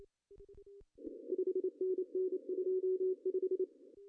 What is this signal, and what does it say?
Morse code, heard as a keyed tone, about 26 wpm, 370 Hz. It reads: 5NN 25
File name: CW / cqww-cw-k5tr